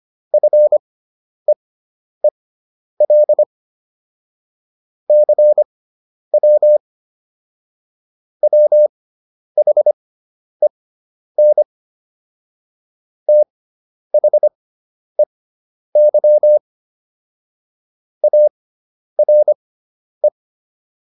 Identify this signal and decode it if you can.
Morse code, heard as a keyed tone, about 25 wpm, spaced as for 10 wpm, 600 Hz. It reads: FEEL CW WHEN THEY ARE